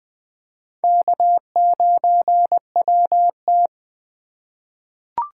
Morse code, heard as a keyed tone, about 20 wpm, 700 Hz. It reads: K9WT
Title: USA Call Signs - 4 Characters - 3x Character Spacing - 20wpm